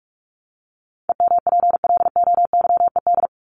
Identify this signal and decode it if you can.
Morse code, heard as a keyed tone, about 44 wpm, 700 Hz. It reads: EMPLOYED
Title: ED Suffix - Top 100 Words with Base Words 45wpm